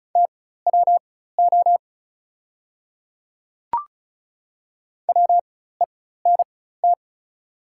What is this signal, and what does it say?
Morse code, heard as a keyed tone, about 34 wpm, spaced as for 16 wpm, 700 Hz. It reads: TWO WENT
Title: 4x Character Spacing - Top 300 Words - 35wpm